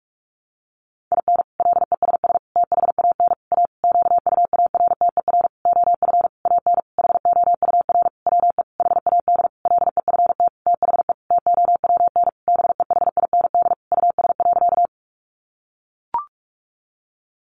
Morse code, heard as a keyed tone, about 44 wpm, 700 Hz. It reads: IN LESS THAN A QUARTER OF AN HOUR WE HAD LEFT THE TOWN BEHIND US.
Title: Sentences from Top 1300 Words 45wpm